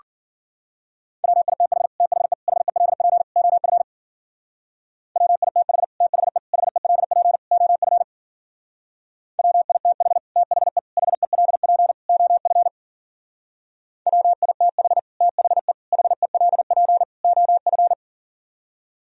Morse code, heard as a keyed tone, about 48 wpm, 700 Hz. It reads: WITH THE HELP OF WITH THE HELP OF WITH THE HELP OF WITH THE HELP OF
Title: Speed-Racing - Sets of 4 Words 40wpm